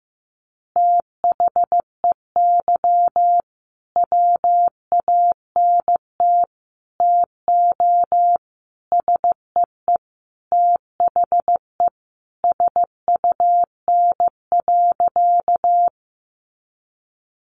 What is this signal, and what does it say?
Morse code, heard as a keyed tone, about 15 wpm, 700 Hz. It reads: THEY WANT TO SEE THE SUN.
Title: Easy Phrases 15wpm